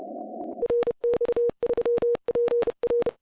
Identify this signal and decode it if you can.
Morse code, heard as a keyed tone, about 28 wpm, 470 Hz. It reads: RX3PR